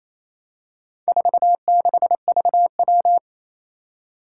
Morse code, heard as a keyed tone, about 27 wpm, 700 Hz. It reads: 46VW
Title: Four Challenging Letter-Number Combinations 28wpm